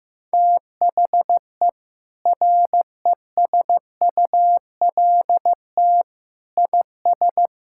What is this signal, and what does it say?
Morse code, heard as a keyed tone, about 15 wpm, 700 Hz. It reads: THE RESULT IS
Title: Sentences from Top 1000 Words - Review Part 5 - 15wpm